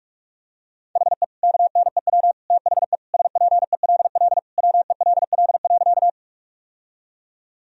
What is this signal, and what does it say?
Morse code, heard as a keyed tone, about 45 wpm, 700 Hz. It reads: HE KNEW THE SPELL WELL.
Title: Sentences from Top 1000 Words - Review Part 1 - 45wpm